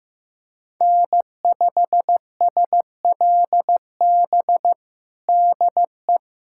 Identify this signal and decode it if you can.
Morse code, heard as a keyed tone, about 15 wpm, 700 Hz. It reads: N5SLB DE
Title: USA Callsign DE Callsign 15wpm